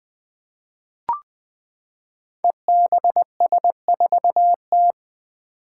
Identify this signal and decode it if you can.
Morse code, heard as a keyed tone, about 20 wpm, 700 Hz. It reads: EBS4T
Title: Five Challenging Letter-Number Combinations 20wpm